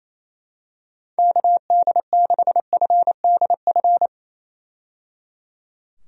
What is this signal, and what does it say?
Morse code, heard as a keyed tone, about 28 wpm, 700 Hz. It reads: KD6FDF
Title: USA Call Signs - 6 Characters - 28wpm